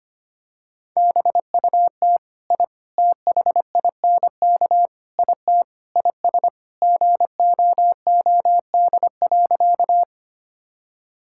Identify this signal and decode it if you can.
Morse code, heard as a keyed tone, about 25 wpm, 700 Hz. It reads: BUT I THINK IT IS GOOD.
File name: practice / 25wpm / Sentences with 4-Word Phrases - 25wpm